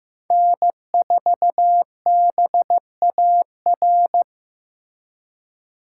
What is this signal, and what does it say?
Morse code, heard as a keyed tone, about 15 wpm, 700 Hz. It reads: N4BAR
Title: USA Call Signs - 4x Character Spacing - 15wpm